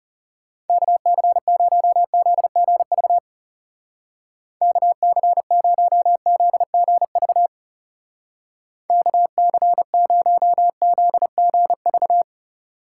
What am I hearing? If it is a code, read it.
Morse code, heard as a keyed tone, about 36 wpm, 700 Hz. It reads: KC0ZGV KC0ZGV KC0ZGV
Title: Speed Racing - USA 2x3 Call Signs 30wpm